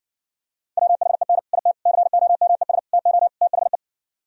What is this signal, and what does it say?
Morse code, heard as a keyed tone, about 60 wpm, 700 Hz. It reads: WHEN IT COMES TO THE